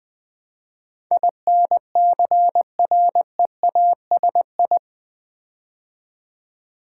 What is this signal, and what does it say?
Morse code, heard as a keyed tone, about 20 wpm, 700 Hz. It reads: INCREASI
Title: Build A Word - 4 Letters At A Time - 12-Letter Words - 20wpm